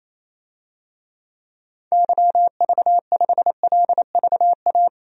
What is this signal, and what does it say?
Morse code, heard as a keyed tone, about 28 wpm, 700 Hz. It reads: YV5LVA
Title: DXCC-100 Call Signs - 6 Characters - 28wpm